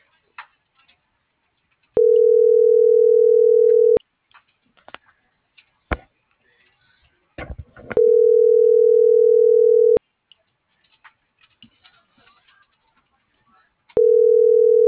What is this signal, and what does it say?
Ringback tone (North America), 440 + 480 Hz.